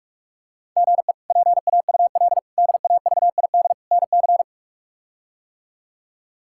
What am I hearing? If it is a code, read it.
Morse code, heard as a keyed tone, about 42 wpm, 700 Hz. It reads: GE PAUL DAVID NC
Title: K1USN SST QSO Practice - 45wpm